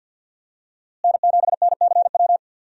Morse code, heard as a keyed tone, about 52 wpm, 700 Hz. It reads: N7NCW
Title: USA Call Signs - 5 Characters - Repeated 2x - 50wpm